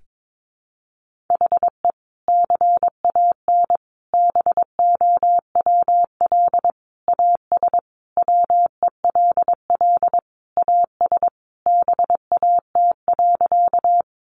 Morse code, heard as a keyed tone, about 22 wpm, 700 Hz. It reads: HE CAN BOWL AS WELL AS BAT.
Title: Sentences from Top 1500 Words 22wpm